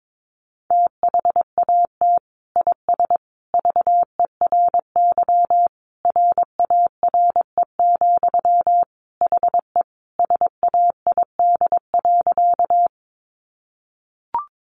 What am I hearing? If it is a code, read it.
Morse code, heard as a keyed tone, about 22 wpm, 700 Hz. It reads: THAT IS VERY RARE, HE SAID.
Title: Sentences from Top 1100 Words 22wpm